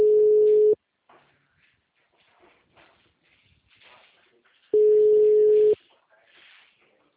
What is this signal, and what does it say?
Ringback tone (Europe, 425 Hz).